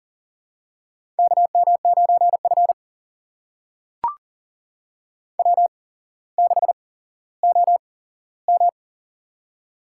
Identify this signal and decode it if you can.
Morse code, heard as a keyed tone, about 41 wpm, spaced as for 17 wpm, 700 Hz. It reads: KM9F W6OM
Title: USA Call Signs - 4 Characters - 8x Character Spacing - 40wpm